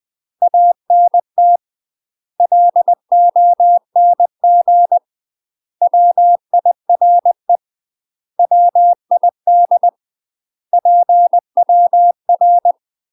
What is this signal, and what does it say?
Morse code, heard as a keyed tone, about 20 wpm, 700 Hz. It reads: ANT LONG WIRE WID PWR
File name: UK MASTER QSOs TRAINING / 20WPM / UK MASTER QSOs 0018